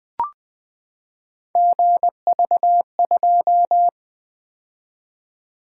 Morse code, heard as a keyed tone, about 20 wpm, 700 Hz. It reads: GV2
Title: Three Letter-Number Combinations - 20wpm